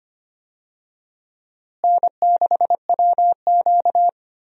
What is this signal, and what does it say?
Morse code, heard as a keyed tone, about 25 wpm, 700 Hz. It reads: N6WQ
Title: USA Call Signs - 4 Characters - Repeated 3x - 25wpm